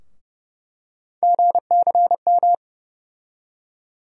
Morse code, heard as a keyed tone, about 30 wpm, 700 Hz. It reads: GCM
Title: USA Call Sign Postfixes - 3 Letters 30wpm